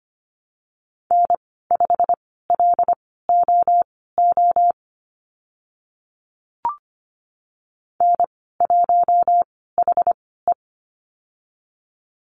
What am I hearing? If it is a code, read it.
Morse code, heard as a keyed tone, about 25 wpm, spaced as for 16 wpm, 700 Hz. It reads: N5LOO N1HE
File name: Lesson 15g - Number 5 - Call Signs 25wpm